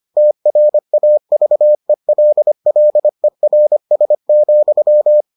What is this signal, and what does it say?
Morse code, heard as a keyed tone, about 25 wpm, 600 Hz. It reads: TRAVELLERS,